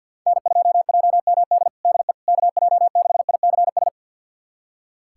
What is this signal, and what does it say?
Morse code, heard as a keyed tone, about 45 wpm, 700 Hz. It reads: N2JMD DE KJ6ICS
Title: USA Callsign DE Callsign 50wpm